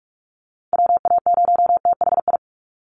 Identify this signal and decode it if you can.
Morse code, heard as a keyed tone, about 44 wpm, 700 Hz. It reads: WA0THI
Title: USA Call Signs 45wpm